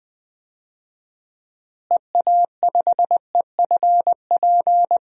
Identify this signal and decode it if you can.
Morse code, heard as a keyed tone, about 20 wpm, 700 Hz. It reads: EA5EFP
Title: DXCC-100 Call Signs - 6 Characters - Repeated 2x - 20wpm